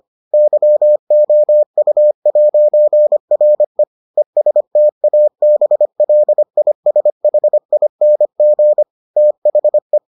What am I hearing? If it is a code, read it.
Morse code, heard as a keyed tone, about 25 wpm, 600 Hz. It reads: YOU'RE ESTABLISHING THE